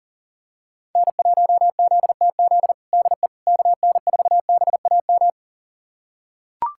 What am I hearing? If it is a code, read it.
Morse code, heard as a keyed tone, about 37 wpm, 700 Hz. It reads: N1ZTZ DE KN4BAM
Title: USA Callsign DE Callsign 40wpm